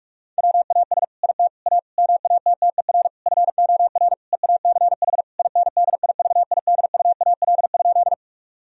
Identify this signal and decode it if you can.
Morse code, heard as a keyed tone, about 46 wpm, 700 Hz. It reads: WAS IT A MATTER FOR EACH INDIVIDUAL?
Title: Sentences from Top 1000 Words - Review Part 4 - 45wpm